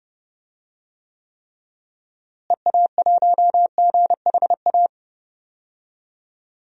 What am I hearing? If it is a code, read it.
Morse code, heard as a keyed tone, about 30 wpm, 700 Hz. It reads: EA1GHA
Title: DXCC-100 Call Signs - 6 Characters - Repeated 3x - 30wpm